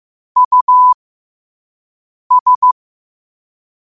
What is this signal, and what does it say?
Morse code, heard as a keyed tone, about 15 wpm, spaced as for 5 wpm, 1000 Hz. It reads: US
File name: 5wpm